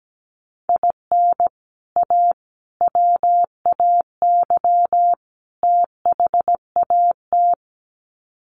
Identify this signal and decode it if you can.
Morse code, heard as a keyed tone, about 17 wpm, 700 Hz. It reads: IN A WAY THAT
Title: Sets of 4 Words 17wpm